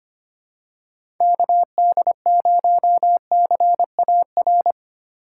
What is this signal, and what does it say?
Morse code, heard as a keyed tone, about 25 wpm, 700 Hz. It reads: KD0CAR
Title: USA Call Signs - 6 Characters - 2x Character Spacing - 25wpm